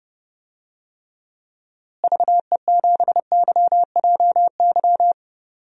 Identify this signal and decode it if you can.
Morse code, heard as a keyed tone, about 31 wpm, 700 Hz. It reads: VE7YJY
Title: DXCC-100 Call Signs - 6 Characters - Repeated 2x - 30wpm